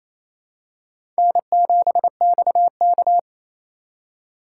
Morse code, heard as a keyed tone, about 28 wpm, 700 Hz. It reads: N7XK
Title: USA Call Signs - 3x Character Spacing - 28wpm